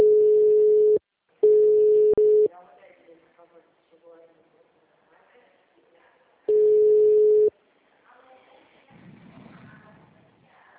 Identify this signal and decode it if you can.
Ringback tone (Europe, 425 Hz), with speech.
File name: out- -2002- - - .19668